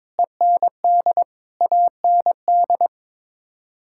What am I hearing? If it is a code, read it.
Morse code, heard as a keyed tone, about 22 wpm, 700 Hz. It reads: END AND
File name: Sets of 9 Words - 22wpm